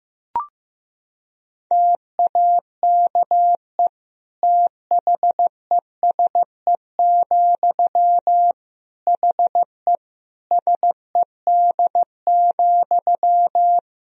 Morse code, heard as a keyed tone, about 15 wpm, 700 Hz. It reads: TAKE THESE, HE SED,
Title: Sentences from Top 300 Words and Abbreviations 15wpm